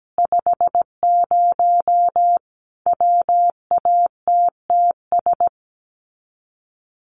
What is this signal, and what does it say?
Morse code, heard as a keyed tone, about 17 wpm, 700 Hz. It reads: 50 WATTS
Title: Common QSO Phrases 17wpm